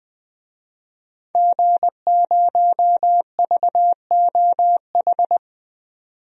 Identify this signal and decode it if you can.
Morse code, heard as a keyed tone, about 20 wpm, 700 Hz. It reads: G0VOH
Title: DXCC-100 Call Signs - 5 Characters - 20wpm